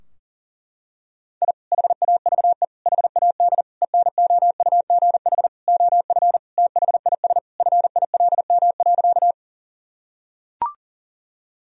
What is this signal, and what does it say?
Morse code, heard as a keyed tone, about 40 wpm, 700 Hz. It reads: I HAVE HAD ENOUGH OF THIS FILM.
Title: Sentences from Top 1000 Words - Review Part 3 - 40wpm